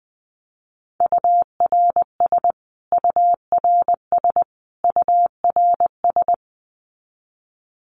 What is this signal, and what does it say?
Morse code, heard as a keyed tone, about 20 wpm, 700 Hz. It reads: URS URS URS
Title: Abbreviations Repeated 3x - 20wpm